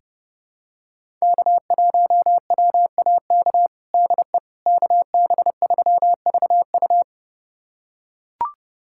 Morse code, heard as a keyed tone, about 30 wpm, 700 Hz. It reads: K1WAK DE KB3VU
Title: USA Callsign DE Callsign - 5 Characters - 30wpm